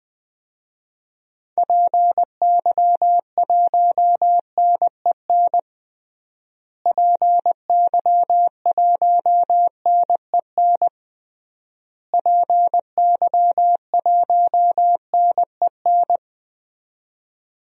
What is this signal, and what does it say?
Morse code, heard as a keyed tone, about 20 wpm, 700 Hz. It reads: PY1NEN PY1NEN PY1NEN
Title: DXCC-100 Call Signs - 6 Characters - Repeated 3x - 20wpm